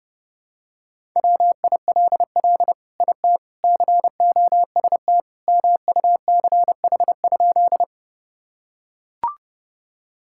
Morse code, heard as a keyed tone, about 32 wpm, 700 Hz. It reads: WILL IT COST MUCH?
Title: Sentences from Top 1000 Words - Review Part 5 - 30wpm